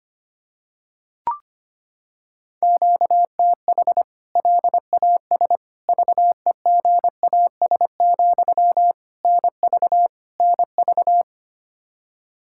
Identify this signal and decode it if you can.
Morse code, heard as a keyed tone, about 25 wpm, 700 Hz. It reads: QTH LAS VEGAS, NV NV
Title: K1USN SST QSO Practice 25wpm